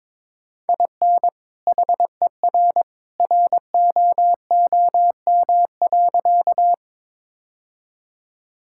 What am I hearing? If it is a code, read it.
Morse code, heard as a keyed tone, about 22 wpm, 700 Hz. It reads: IN HER ROOM.
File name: practice / 22wpm / Sentences with 2-Word Phrases - 22wpm